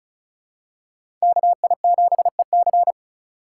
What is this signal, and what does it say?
Morse code, heard as a keyed tone, about 36 wpm, 700 Hz. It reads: KI7EC
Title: USA Call Signs - 3x Character Spacing - 35wpm